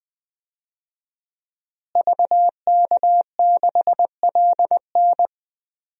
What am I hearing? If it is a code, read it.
Morse code, heard as a keyed tone, about 20 wpm, 700 Hz. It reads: VK6LN
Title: DXCC-100 Call Signs - 5 Characters - Repeated 3x - 20wpm